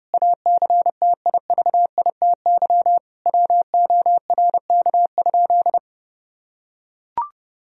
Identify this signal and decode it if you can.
Morse code, heard as a keyed tone, about 30 wpm, 700 Hz. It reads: ACTIVITY WORK?
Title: Sentences from Top 1000 Words - Review Part 4 - 30wpm